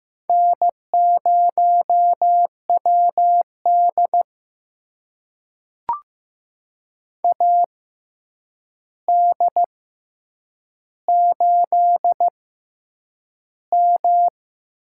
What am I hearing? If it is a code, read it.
Morse code, heard as a keyed tone, about 15 wpm, spaced as for 8 wpm, 700 Hz. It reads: N0WD AD8M
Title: USA Call Signs - 4 Characters - 6x Character Spacing - 15wpm